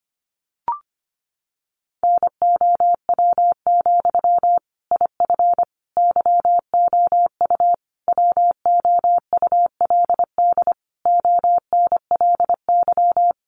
Morse code, heard as a keyed tone, about 25 wpm, 700 Hz. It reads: NOW, IF YOU WOULD ONLY
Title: Sentences from Top 900 Words 25wpm